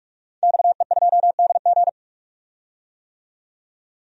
Morse code, heard as a keyed tone, about 46 wpm, 700 Hz. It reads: XE2DG
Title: DXCC-100 Call Signs - 5 Characters - 45wpm